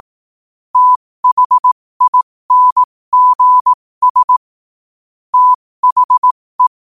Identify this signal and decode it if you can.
Morse code, heard as a keyed tone, about 18 wpm, spaced as for 15 wpm, 1000 Hz. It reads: THINGS THE
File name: qotdfiles / 15wpm